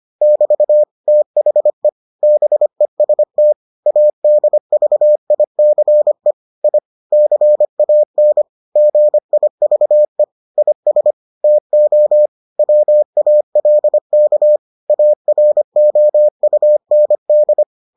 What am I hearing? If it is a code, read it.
Morse code, heard as a keyed tone, about 25 wpm, 600 Hz. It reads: <BT> THE BEST ADVICE I CAN GIVE IS TO WALK AROUND